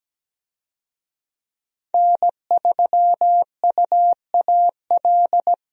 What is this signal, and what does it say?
Morse code, heard as a keyed tone, about 17 wpm, 700 Hz. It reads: N3UAL
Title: USA Call Signs - 5 Characters - 17wpm